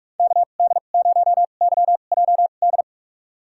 Morse code, heard as a keyed tone, about 46 wpm, spaced as for 32 wpm, 700 Hz. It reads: KD0YJD
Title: USA Call Signs - 2x Character Spacing - 45wpm